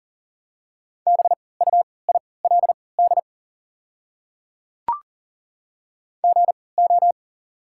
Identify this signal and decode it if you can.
Morse code, heard as a keyed tone, about 40 wpm, spaced as for 23 wpm, 690 Hz. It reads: BUILD GO
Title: 3x Character Spacing - Top 200 Words - 40wpm